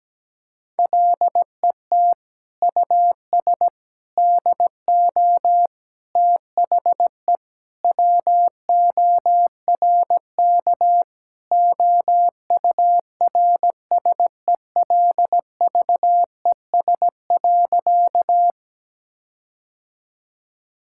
Morse code, heard as a keyed tone, about 17 wpm, 700 Hz. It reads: LET US DO THE WORK OURSELVES.
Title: Sentences from Top 1600 Words 17wpm